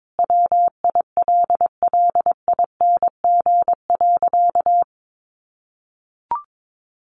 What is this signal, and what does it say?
Morse code, heard as a keyed tone, about 22 wpm, 700 Hz. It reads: WILLING.
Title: Sentences from Top 1400 Words 22wpm